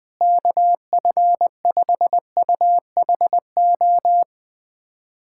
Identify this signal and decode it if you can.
Morse code, heard as a keyed tone, about 21 wpm, 700 Hz. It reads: KF5UHO
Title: USA Callsign DE Callsign - 6 Characters - 20wpm